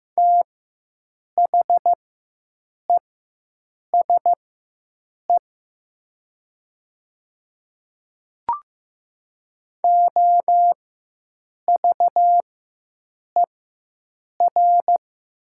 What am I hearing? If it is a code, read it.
Morse code, heard as a keyed tone, about 15 wpm, spaced as for 7 wpm, 700 Hz. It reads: THESE OVER
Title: 4x Character Spacing - Top 100 Words - 15wpm